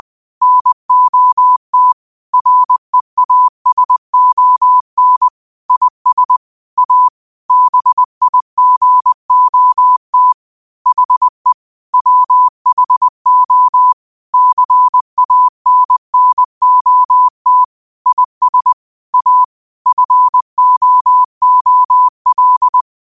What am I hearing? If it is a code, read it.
Morse code, heard as a keyed tone, about 20 wpm, 1000 Hz. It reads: NOT REASON IS A BIGOT HE WHO CANNOT IS A FOOL
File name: qotdfiles / 20wpm